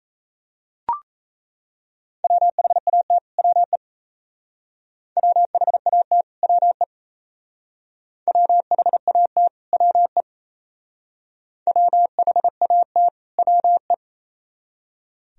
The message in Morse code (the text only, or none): WHAT WE WHAT WE WHAT WE WHAT WE